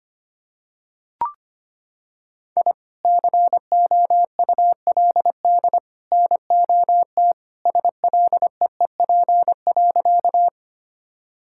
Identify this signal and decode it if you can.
Morse code, heard as a keyed tone, about 25 wpm, 700 Hz. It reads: I COULD NOT SLEEP.
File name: practice / 25wpm / Sentences with 3-Word Phrases - 25wpm